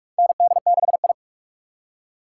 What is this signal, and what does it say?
Morse code, heard as a keyed tone, about 44 wpm, 700 Hz. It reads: ND6I
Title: USA Call Signs - 4 Characters - 3x Character Spacing - 45wpm